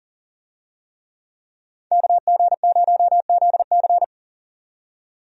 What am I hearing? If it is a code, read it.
Morse code, heard as a keyed tone, about 40 wpm, 700 Hz. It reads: KG0ZC
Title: USA Call Signs - 5 Characters - 40wpm